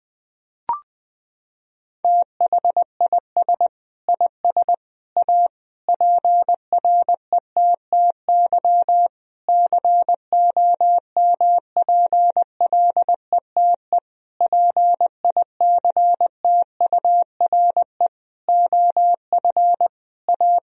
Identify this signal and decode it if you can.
Morse code, heard as a keyed tone, about 21 wpm, 700 Hz. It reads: THIS IS A PRETTY COMPLETE PICTURE OF A
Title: Sentences from Top 1000 Words - Review Part 2 - 20wpm